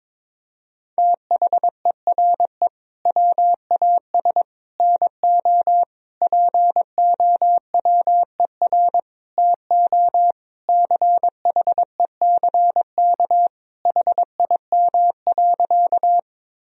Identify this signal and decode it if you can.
Morse code, heard as a keyed tone, about 22 wpm, 700 Hz. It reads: THERE WAS NO POWER TO CHECK HIM.
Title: Sentences from Top 1000 Words - Review Part 3 - 22wpm